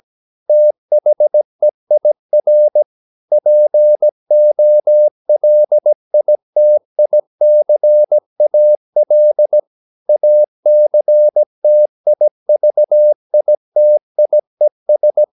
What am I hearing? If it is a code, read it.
Morse code, heard as a keyed tone, about 17 wpm, 600 Hz. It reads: THEIR POLITICAL ACTIVITIES
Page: Romanen Never af Ken Follet på engelsk.